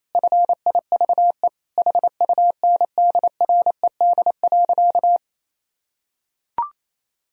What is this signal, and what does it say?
Morse code, heard as a keyed tone, about 28 wpm, 700 Hz. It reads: FIVE HUNDRED.
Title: Sentences from Top 1000 Words - Review Part 2 - 28wpm